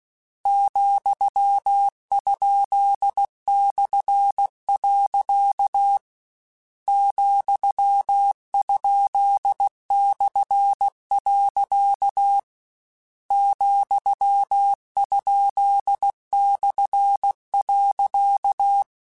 Morse code, heard as a keyed tone, about 16 wpm, 800 Hz. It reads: ,?/. ,?/. ,?/.
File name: Misc / Misc-mp3s / Punct @15